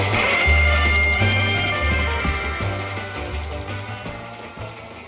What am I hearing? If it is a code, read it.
Dial-up modem handshake.